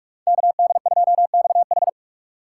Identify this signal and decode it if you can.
Morse code, heard as a keyed tone, about 46 wpm, 700 Hz. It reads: KD2XH